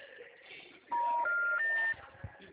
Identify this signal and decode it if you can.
Special information tone.